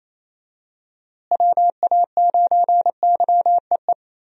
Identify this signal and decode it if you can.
Morse code, heard as a keyed tone, about 28 wpm, 700 Hz. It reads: WA9YEE
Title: USA Call Signs - 6 Characters - Repeated 3x - 28wpm